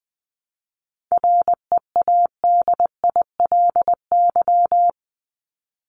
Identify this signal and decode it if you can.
Morse code, heard as a keyed tone, about 20 wpm, 700 Hz. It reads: READILY
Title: LY Suffix - Top 100 Words with Base Words 20wpm